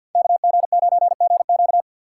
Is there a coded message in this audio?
Yes: Morse code from 0.1 to 1.8 s, heard as a keyed tone, about 64 wpm, 700 Hz. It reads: KG9GQ